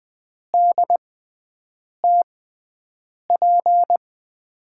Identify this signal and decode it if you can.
Morse code, heard as a keyed tone, about 21 wpm, spaced as for 7 wpm, 700 Hz. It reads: DTP
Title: USA Call Signs - 5 Characters - 6x Character Spacing - 20wpm